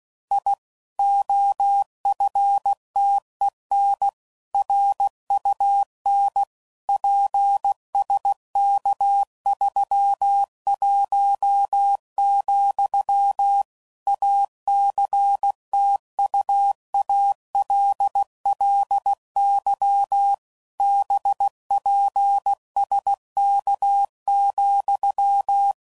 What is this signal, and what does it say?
Morse code, heard as a keyed tone, about 16 wpm, 800 Hz. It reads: I OFTEN RUN PSK31, ACTUALLY BPSK,
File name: CW Code Groups @15WPM / QSOs / WA7QOX QRX